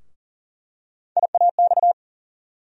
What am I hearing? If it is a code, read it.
Morse code, heard as a keyed tone, about 40 wpm, 700 Hz. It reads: IAX
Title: USA Call Sign Postfixes - 3 Letters 40wpm